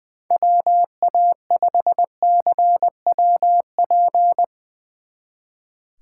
Morse code, heard as a keyed tone, about 21 wpm, 700 Hz. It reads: WA5CWP
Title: Speed Racing - USA Call Signs 20wpm